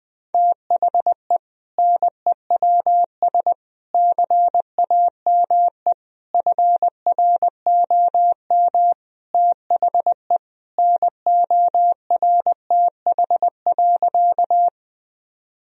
Morse code, heard as a keyed tone, about 20 wpm, 700 Hz. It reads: THE NEWS CAME FROM THE NORTH.
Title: Sentences from Top 1000 Words - Review Part 1 - 20wpm